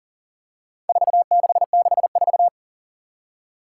Morse code, heard as a keyed tone, about 37 wpm, 700 Hz. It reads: 4664
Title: Four Challenging Letter-Number Combinations 40wpm